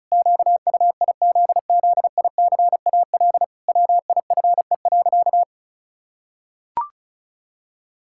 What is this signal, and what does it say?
Morse code, heard as a keyed tone, about 36 wpm, 700 Hz. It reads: QUIZZICAL WIFE.
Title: Pangram Challenge v2 - 35wpm